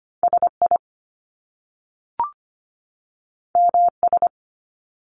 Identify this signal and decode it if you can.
Morse code, heard as a keyed tone, about 25 wpm, 700 Hz. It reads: SI MS
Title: Top Two Letter Combinations in Words 25wpm